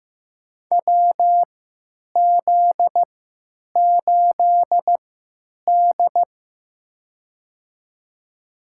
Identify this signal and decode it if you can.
Morse code, heard as a keyed tone, about 15 wpm, spaced as for 9 wpm, 700 Hz. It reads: WZ8D
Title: USA Call Signs - 4 Characters - 3x Character Spacing - 15wpm